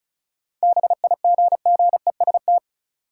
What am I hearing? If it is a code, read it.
Morse code, heard as a keyed tone, about 36 wpm, 700 Hz. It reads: BIGGEST
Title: Sets of 4 Words - New Word Drill - 35wpm